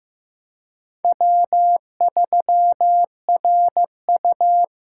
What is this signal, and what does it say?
Morse code, heard as a keyed tone, about 16 wpm, 700 Hz. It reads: W3RU
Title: USA Call Signs - 4 Characters - 3x Character Spacing - 15wpm